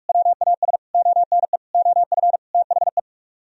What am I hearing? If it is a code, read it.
Morse code, heard as a keyed tone, about 50 wpm, 700 Hz. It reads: WAS ONE OF THE